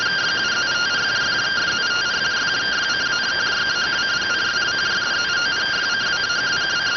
Two-tone teleprinter signal, about 170 Hz shift.